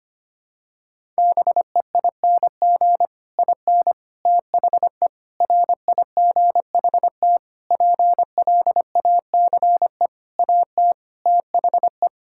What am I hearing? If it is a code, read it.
Morse code, heard as a keyed tone, about 25 wpm, 700 Hz. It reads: BEING IN THE RIGHT PLACE AT THE
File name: Sets of 9 Words - 25wpm